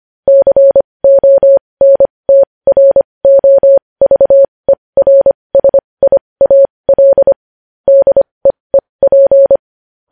Morse code, heard as a keyed tone, about 25 wpm, spaced as for 20 wpm, 550 Hz. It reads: CONTROVERSIAL DEEP
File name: podcast / News headlines-1x-25x20